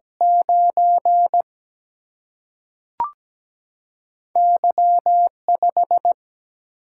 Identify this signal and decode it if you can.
Morse code, heard as a keyed tone, about 17 wpm, 700 Hz. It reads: 9 Y5
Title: Single-Letter Single-Number 17wpm